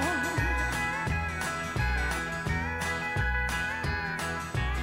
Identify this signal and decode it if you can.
Emergency Alert System (SAME) tones, heard as harsh buzzing bursts. It is heard from 0.5 to 3.8 s.